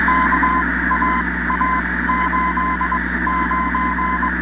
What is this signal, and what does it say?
Morse code, heard as a keyed tone, about 22 wpm, spaced as for 17 wpm, 1020 Hz, with noise. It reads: KAA89